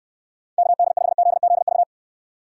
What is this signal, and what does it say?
Morse code, heard as a keyed tone, about 64 wpm, 700 Hz. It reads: KD5ZZV